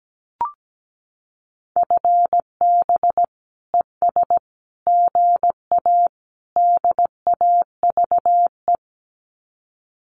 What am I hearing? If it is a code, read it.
Morse code, heard as a keyed tone, about 17 wpm, 700 Hz. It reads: FB ES GA DAVE
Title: Common QSO Phrases Encore 17wpm